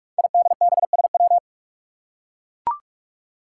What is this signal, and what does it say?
Morse code, heard as a keyed tone, about 44 wpm, 700 Hz. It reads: IDBSW
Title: Five Challenging Letter-Number Combinations 45wpm